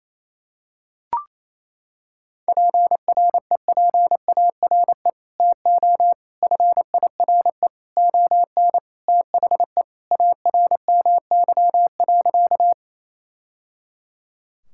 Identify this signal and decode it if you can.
Morse code, heard as a keyed tone, about 28 wpm, 700 Hz. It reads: PREPARE TO FIRE ON THE ARMY.
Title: Sentences from Top 1000 Words - Review Part 6 - 28wpm